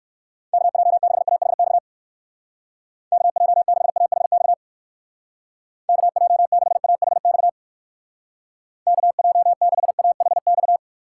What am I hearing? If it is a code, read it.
Morse code, heard as a keyed tone, about 59 wpm, 700 Hz. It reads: KJ6AHX KJ6AHX KJ6AHX KJ6AHX